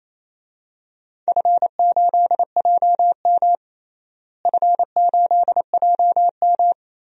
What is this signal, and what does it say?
Morse code, heard as a keyed tone, about 28 wpm, 700 Hz. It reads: F8JM F8JM
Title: DXCC-100 Call Signs - 4 Characters - Repeated 2x - 28wpm